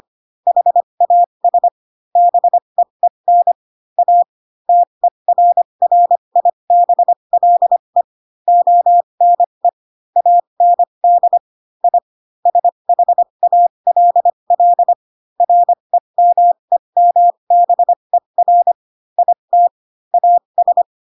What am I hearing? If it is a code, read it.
Morse code, heard as a keyed tone, about 25 wpm, spaced as for 22 wpm, 700 Hz. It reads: HAS BEEN A TERRIBLE ONE AND I SHALL REMEMBER IT AS
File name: Pinocchio / Pinocchio ch7 18 25 ik1hge